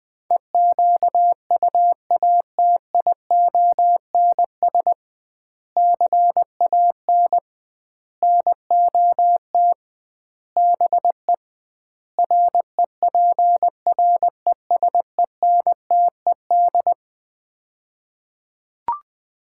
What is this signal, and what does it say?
Morse code, heard as a keyed tone, about 21 wpm, 700 Hz. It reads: EQUATIONS CAN NOT BE REPRESENTED
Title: 2x Word-Spacing - Sets of 5 Words - 20wpm